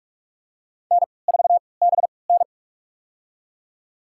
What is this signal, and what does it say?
Morse code, heard as a keyed tone, about 46 wpm, spaced as for 26 wpm, 700 Hz. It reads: N4BN
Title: USA Call Signs - 4 Characters - 3x Character Spacing - 45wpm